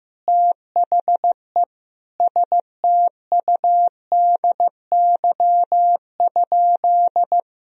Morse code, heard as a keyed tone, about 15 wpm, 700 Hz. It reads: THE STUDY?
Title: Sentences from Top 1000 Words - Review Part 2 - 15wpm